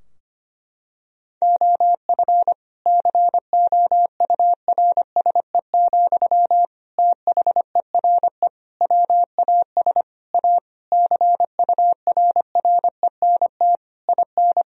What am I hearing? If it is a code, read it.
Morse code, heard as a keyed tone, about 25 wpm, 700 Hz. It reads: OF COURSE, THERE WAS A CURRENT IN
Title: Sentences from Top 1000 Words - Review Part 4 - 25wpm